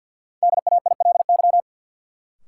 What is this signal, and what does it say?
Morse code, heard as a keyed tone, about 50 wpm, 700 Hz. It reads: DAILY
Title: Sentences from Top 1100 Words - New Word Drill - 50wpm